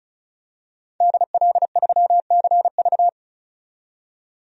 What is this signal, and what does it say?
Morse code, heard as a keyed tone, about 36 wpm, 700 Hz. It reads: DL3CV
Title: DXCC-100 Call Signs - 5 Characters - 35wpm